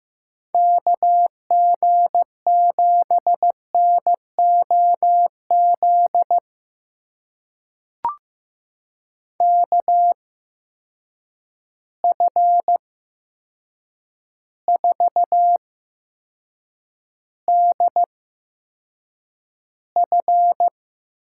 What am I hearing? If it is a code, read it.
Morse code, heard as a keyed tone, about 15 wpm, spaced as for 7 wpm, 700 Hz. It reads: KG7NOZ KF4DF
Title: USA Call Signs - 6 Characters - 8x Character Spacing - 15wpm